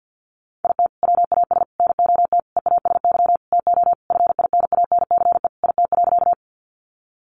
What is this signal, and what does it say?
Morse code, heard as a keyed tone, about 51 wpm, 700 Hz. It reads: IT WAS NOT EASY TO FINANCE IT.
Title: Sentences from Top 1500 Words 50wpm